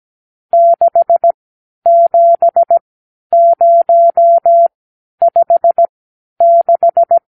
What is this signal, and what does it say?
Morse code, heard as a keyed tone, about 17 wpm, spaced as for 10 wpm, 680 Hz. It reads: 67056
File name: arrl / Review (1-0)